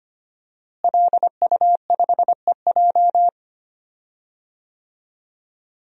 Morse code, heard as a keyed tone, about 25 wpm, 700 Hz. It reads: LU5EJ
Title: DXCC-100 Call Signs - 5 Characters - Repeated 2x - 25wpm